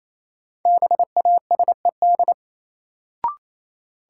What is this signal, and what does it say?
Morse code, heard as a keyed tone, about 28 wpm, 700 Hz. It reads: BASED
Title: Sets of 2 Words - New Word Drill - 28wpm